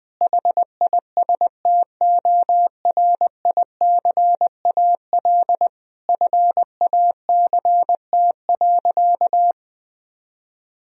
Morse code, heard as a keyed tone, about 20 wpm, 700 Hz. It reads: HISTORICAL FACT.
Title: Sentences from Top 1000 Words - Review Part 4 - 20wpm